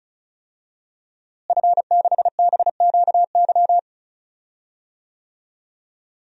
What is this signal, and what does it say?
Morse code, heard as a keyed tone, about 34 wpm, 700 Hz. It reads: F6BQY
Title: DXCC-100 Call Signs - 5 Characters - Repeated 3x - 35wpm